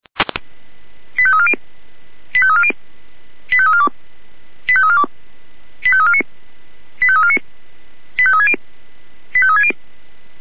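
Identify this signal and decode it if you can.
Five-tone selective call (ZVEI), heard as quick runs of five tones, tones of about 70 ms. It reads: 85340 85340 85342 85342 85349 85349 85360 85360
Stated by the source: ZVEI five-tone selective call received on SDR.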